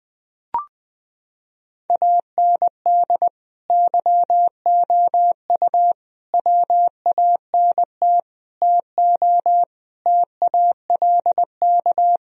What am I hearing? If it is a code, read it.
Morse code, heard as a keyed tone, about 20 wpm, 700 Hz. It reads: AND YOU WANT TO TALK
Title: Sentences from Top 1000 Words - Review Part 4 - 20wpm